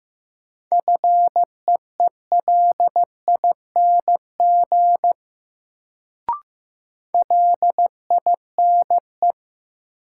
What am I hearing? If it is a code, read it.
Morse code, heard as a keyed tone, about 15 wpm, 700 Hz. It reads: FEELING LINE
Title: Sentences from Top 1200 Words - New Word Drill Alt - 15wpm